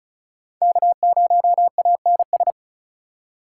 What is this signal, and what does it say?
Morse code, heard as a keyed tone, about 36 wpm, 700 Hz. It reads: K0ANS
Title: USA Call Signs - 5 Characters - 2x Character Spacing - 35wpm